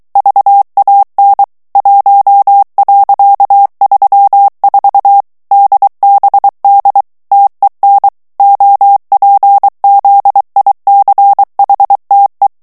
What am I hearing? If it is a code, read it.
Morse code, heard as a keyed tone, about 23 wpm, 800 Hz. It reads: VAN 1.34 DBD TEN OPZICHTE
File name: HOLLANDAIS texte clair 1 vitesse 22 mots minute